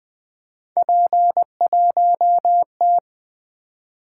Morse code, heard as a keyed tone, about 20 wpm, 700 Hz. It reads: P1T
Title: Three Letter-Number Combinations - 20wpm